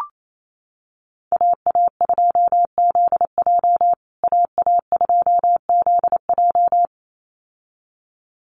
Morse code, heard as a keyed tone, about 28 wpm, 700 Hz. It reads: AA2ZJ AA2ZJ
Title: USA Call Signs Repeated 2x 28wpm